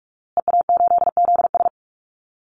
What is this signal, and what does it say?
Morse code, heard as a keyed tone, about 44 wpm, 700 Hz. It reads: EA8ZS
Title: Call Signs Repeated 2x - 45wpm